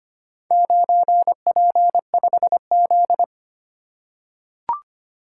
Morse code, heard as a keyed tone, about 25 wpm, 700 Hz. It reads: 9P5Z
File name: practice / 25wpm / Four Letter-Number Combinations - 25wpm